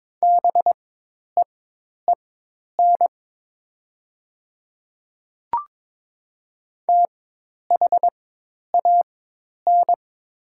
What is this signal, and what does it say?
Morse code, heard as a keyed tone, about 22 wpm, spaced as for 10 wpm, 700 Hz. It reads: BEEN THAN
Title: 4x Character Spacing - Top 300 Words - 22wpm